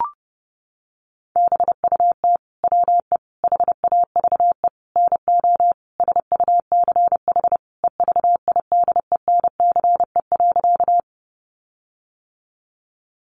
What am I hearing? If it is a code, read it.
Morse code, heard as a keyed tone, about 30 wpm, 700 Hz. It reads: BUT WE HAVE NO SUCH EVIDENCE.
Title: Sentences from Top 1300 Words 30wpm